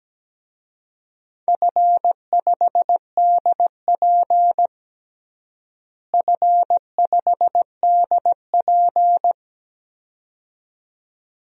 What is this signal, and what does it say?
Morse code, heard as a keyed tone, about 17 wpm, 700 Hz. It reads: F5DP F5DP
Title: DXCC-100 Call Signs - 4 Characters - Repeated 2x - 17wpm